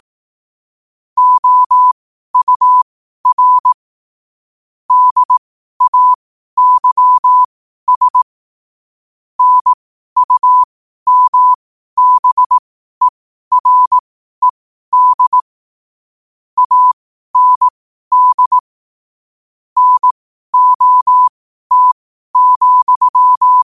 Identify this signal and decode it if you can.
Morse code, heard as a keyed tone, about 18 wpm, spaced as for 12 wpm, 1000 Hz. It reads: OUR DAYS NUMBERED AND NOT,
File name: qotdfiles / 12wpm